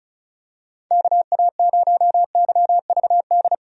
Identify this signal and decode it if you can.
Morse code, heard as a keyed tone, about 36 wpm, 700 Hz. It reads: KA0YVD
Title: USA Call Signs - 6 Characters - 35wpm